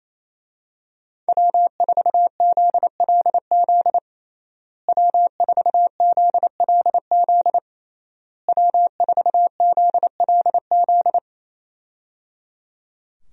Morse code, heard as a keyed tone, about 27 wpm, 700 Hz. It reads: W4ZLZ W4ZLZ W4ZLZ
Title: USA Call Signs - 5 Characters - Repeated 3x - 28wpm